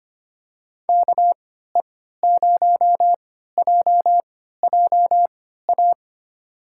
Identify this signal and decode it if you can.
Morse code, heard as a keyed tone, about 25 wpm, spaced as for 14 wpm, 700 Hz. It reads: KE0JJA
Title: USA Call Signs - 6 Characters - 3x Character Spacing - 25wpm